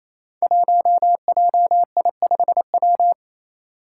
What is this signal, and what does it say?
Morse code, heard as a keyed tone, about 28 wpm, 700 Hz. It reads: 1JI5W
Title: Five Challenging Letter-Number Combinations 28wpm